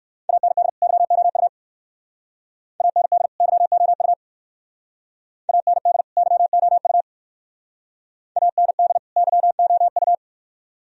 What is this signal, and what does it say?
Morse code, heard as a keyed tone, about 62 wpm, 700 Hz. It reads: AND YOU AND YOU AND YOU AND YOU